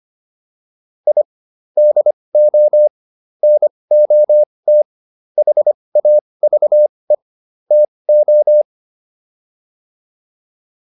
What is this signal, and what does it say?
Morse code, heard as a keyed tone, about 25 wpm, spaced as for 20 wpm, 600 Hz. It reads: I DO NOT HAVE TO